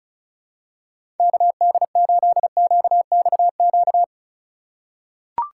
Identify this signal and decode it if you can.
Morse code, heard as a keyed tone, about 34 wpm, 700 Hz. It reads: KD8QXQ
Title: USA Call Signs - 6 Characters - 2x Character Spacing - 35wpm